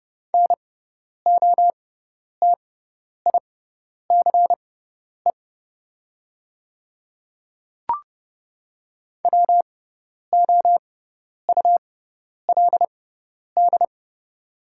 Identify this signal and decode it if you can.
Morse code, heard as a keyed tone, about 29 wpm, spaced as for 10 wpm, 700 Hz. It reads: NOTICE WOULD
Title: 6x Character Spacing - Top 500 Words - 30wpm